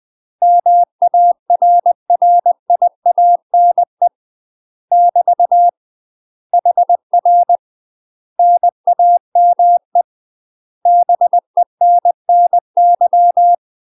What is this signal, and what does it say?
Morse code, heard as a keyed tone, about 20 wpm, 700 Hz. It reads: MARRIANE <BT> HR NAME BENNY
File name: UK MASTER QSOs TRAINING / 20WPM / UK MASTER QSOs 0062